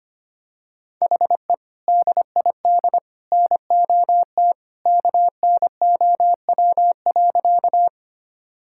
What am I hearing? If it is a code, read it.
Morse code, heard as a keyed tone, about 25 wpm, 700 Hz. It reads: HE DID NOT KNOW.
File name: practice / 25wpm / Sentences with 2-Word Phrases - 25wpm